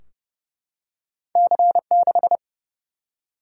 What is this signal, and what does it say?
Morse code, heard as a keyed tone, about 30 wpm, 700 Hz. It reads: C6
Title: Single-Letter Single-Number 30wpm